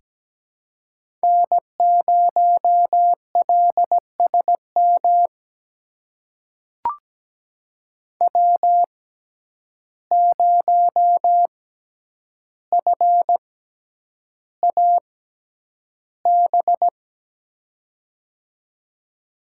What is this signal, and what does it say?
Morse code, heard as a keyed tone, about 17 wpm, spaced as for 9 wpm, 700 Hz. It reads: N0LSM W0FAB
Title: USA Call Signs - 5 Characters - 6x Character Spacing - 17wpm